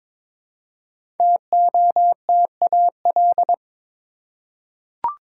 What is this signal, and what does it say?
Morse code, heard as a keyed tone, about 22 wpm, 700 Hz. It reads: TOTAL
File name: practice / 22wpm / 5 Letter Words - First 3 Letters Given - 22wpm